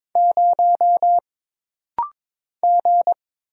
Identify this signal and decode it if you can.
Morse code, heard as a keyed tone, about 22 wpm, 700 Hz. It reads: 0 G
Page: This rapid-fire timing is geared to push your ICR (Instant Character Recognition) skills.